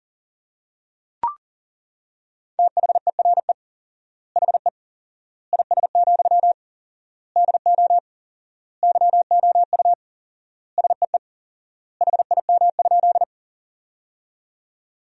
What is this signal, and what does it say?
Morse code, heard as a keyed tone, about 40 wpm, 700 Hz. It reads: THERE HE IS, DO YOU SEE HIM?
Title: 4x Word-Spacing - Sentences from Top 100 Words - 40wpm